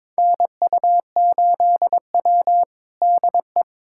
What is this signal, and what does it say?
Morse code, heard as a keyed tone, about 22 wpm, 700 Hz. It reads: NU8W DE
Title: USA Callsign DE Callsign - 4 Characters - 22wpm